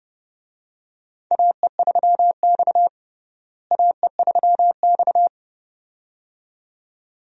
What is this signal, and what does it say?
Morse code, heard as a keyed tone, about 30 wpm, 700 Hz. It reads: AE3X AE3X
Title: USA Call Signs - 4 Characters - Repeated 2x - 30wpm